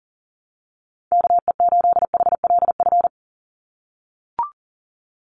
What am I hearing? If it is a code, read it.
Morse code, heard as a keyed tone, about 40 wpm, 700 Hz. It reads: KE8HLF
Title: USA Call Signs 40wpm